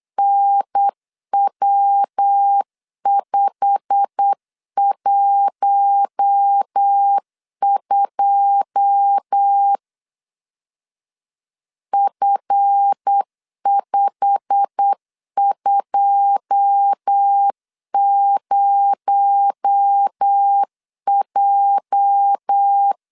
Morse code, heard as a keyed tone, about 8 wpm, 800 Hz. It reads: NW512 F520J
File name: LEZIONI / 9 2 8wpm 3437